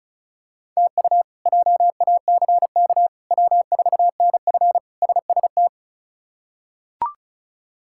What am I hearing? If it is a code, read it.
Morse code, heard as a keyed tone, about 36 wpm, 700 Hz. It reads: TU JACK W4NF SST
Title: K1USN SST QSO Practice - 35wpm